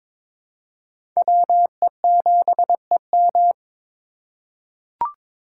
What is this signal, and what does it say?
Morse code, heard as a keyed tone, about 22 wpm, 700 Hz. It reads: WE7EM
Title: USA Call Signs - 4x Character Spacing - 22wpm